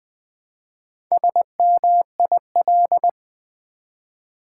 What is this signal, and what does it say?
Morse code, heard as a keyed tone, about 20 wpm, 700 Hz. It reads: SMIL
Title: Build A Word - 1 Letter At A Time - 5-Letter Words - 20wpm